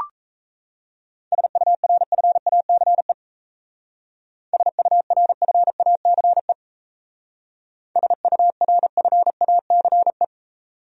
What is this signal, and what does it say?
Morse code, heard as a keyed tone, about 33 wpm, 700 Hz. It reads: SURFACE SURFACE SURFACE
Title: Speed-Racing Top 500 Words 28wpm